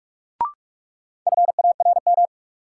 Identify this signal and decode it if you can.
Morse code, heard as a keyed tone, about 46 wpm, 700 Hz. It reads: FARM
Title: Sentences from Top 1200 Words - New Word Drill Alt - 45wpm